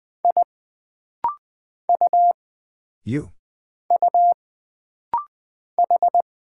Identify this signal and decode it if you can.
Morse code, heard as a keyed tone, about 20 wpm, 700 Hz. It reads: I U U H
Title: Single Character (Letter:Number:Punctuation:Prosign) - Rapid-Fire - 20wpm